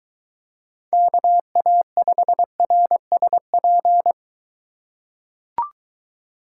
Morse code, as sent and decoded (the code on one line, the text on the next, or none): -.- .- ..... .-. ... .--.
KA5RSP